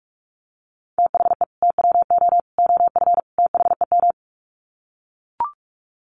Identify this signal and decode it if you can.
Morse code, heard as a keyed tone, about 45 wpm, 700 Hz. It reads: THE TWO OF THEM
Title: Sets of 4 Words 45wpm